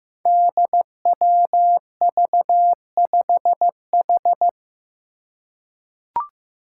Morse code, heard as a keyed tone, about 15 wpm, 700 Hz. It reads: DWV5H
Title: Five Challenging Letter-Number Combinations 15wpm